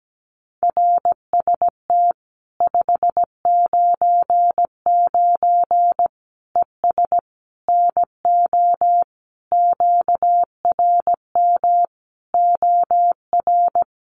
Morse code, heard as a keyed tone, about 17 wpm, 700 Hz. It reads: RST 599 ES NO QRM OR
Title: Common QSO Phrases 17wpm